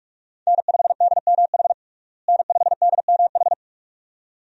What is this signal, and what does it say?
Morse code, heard as a keyed tone, about 46 wpm, 700 Hz. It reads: N5DMH N5DMH
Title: USA Call Signs - 5 Characters - Repeated 2x - 45wpm